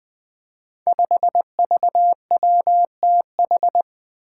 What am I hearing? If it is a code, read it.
Morse code, heard as a keyed tone, about 20 wpm, 700 Hz. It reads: 5VWTH
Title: Five Challenging Letter-Number Combinations 20wpm